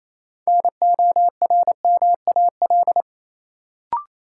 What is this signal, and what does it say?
Morse code, heard as a keyed tone, about 28 wpm, 700 Hz. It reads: NORMAL
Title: Sentences from Top 1200 Words - New Word Drill - 28wpm